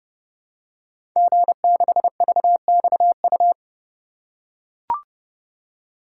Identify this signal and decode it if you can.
Morse code, heard as a keyed tone, about 30 wpm, 700 Hz. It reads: G6VXU
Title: Code Groups - 5 Letter:Numbers - 30wpm